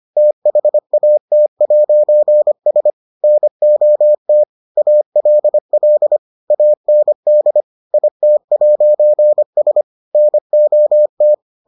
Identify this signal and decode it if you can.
Morse code, heard as a keyed tone, about 25 wpm, 600 Hz. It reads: THAT'S NOT ALL AND IT'S NOT